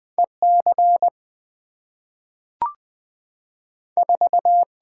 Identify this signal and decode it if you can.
Morse code, heard as a keyed tone, about 20 wpm, 700 Hz. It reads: E C 4
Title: Two Letter-Number Combinations - 20wpm